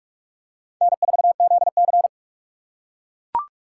Morse code, heard as a keyed tone, about 44 wpm, 700 Hz. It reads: N4ZC
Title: USA Call Signs - 4 Characters - 4x Character Spacing - 45wpm